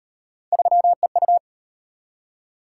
Morse code, heard as a keyed tone, about 33 wpm, 700 Hz. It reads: 3EU